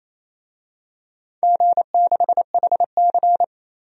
Morse code, heard as a keyed tone, about 29 wpm, 700 Hz. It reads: G6HC
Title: DXCC-100 Call Signs - 4 Characters - 28wpm